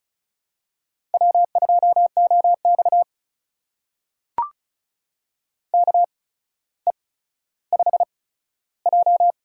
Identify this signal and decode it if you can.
Morse code, heard as a keyed tone, about 36 wpm, spaced as for 15 wpm, 700 Hz. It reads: W2OX KE5J
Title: USA Call Signs - 4 Characters - 8x Character Spacing - 35wpm